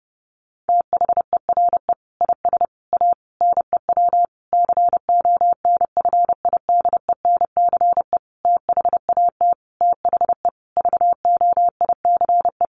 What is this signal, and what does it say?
Morse code, heard as a keyed tone, about 30 wpm, 700 Hz. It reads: THERE IS A NEW CONFIDENCE THAT THE VOICE
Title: Sentences from Top 1500 Words 30wpm